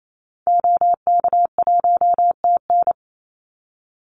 Morse code, heard as a keyed tone, about 28 wpm, 700 Hz. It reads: OK1TN
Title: Call Signs Repeated 3x - 28wpm